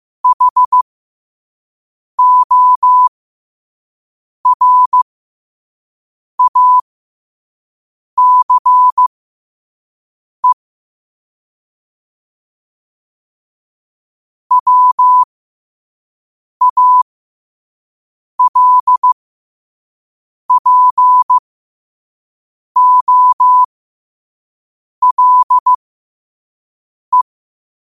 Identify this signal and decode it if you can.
Morse code, heard as a keyed tone, about 15 wpm, spaced as for 5 wpm, 1000 Hz. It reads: HORACE WALPOLE
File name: qotdfiles / 5wpm